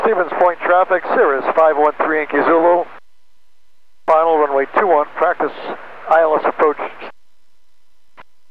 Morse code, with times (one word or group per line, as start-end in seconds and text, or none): none